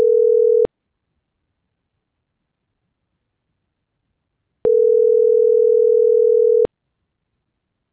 Ringback tone (North America), 440 + 480 Hz.